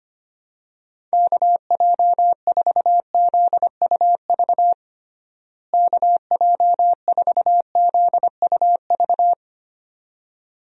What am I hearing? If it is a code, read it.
Morse code, heard as a keyed tone, about 25 wpm, 700 Hz. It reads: KJ4ZUV KJ4ZUV
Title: USA Call Signs - 6 Characters - Repeated 2x - 25wpm